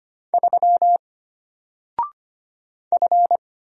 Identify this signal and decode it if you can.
Morse code, heard as a keyed tone, about 25 wpm, 700 Hz. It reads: 3 F
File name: Single-Letter-Number-25wpm